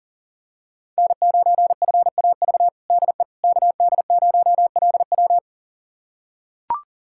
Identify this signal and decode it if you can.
Morse code, heard as a keyed tone, about 40 wpm, 700 Hz. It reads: N9FAV DE KD0LW
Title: USA Callsign DE Callsign - 5 Characters - 40wpm